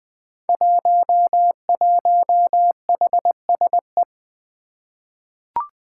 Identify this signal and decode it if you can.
Morse code, heard as a keyed tone, about 20 wpm, 700 Hz. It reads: 11HSE
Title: Five Challenging Letter-Number Combinations 20wpm